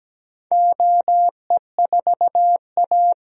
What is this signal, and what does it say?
Morse code, heard as a keyed tone, about 17 wpm, 700 Hz. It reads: OE4A
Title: DXCC-100 Call Signs - 4 Characters - 17wpm